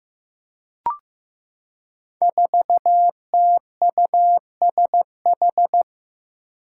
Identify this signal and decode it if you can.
Morse code, heard as a keyed tone, about 16 wpm, 700 Hz. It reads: 4TUSH
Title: Five Challenging Letter-Number Combinations 15wpm